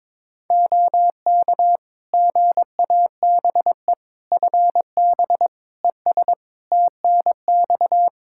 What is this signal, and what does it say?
Morse code, heard as a keyed tone, about 22 wpm, 700 Hz. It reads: OK GABE FB ES TNX
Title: Practice QSO - Exchange 2 - Info - 22wpm